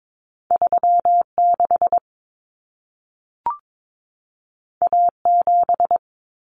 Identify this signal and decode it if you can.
Morse code, heard as a keyed tone, about 22 wpm, 700 Hz. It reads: 36 A7
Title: Common Cut Numbers - 2-Digit 22wpm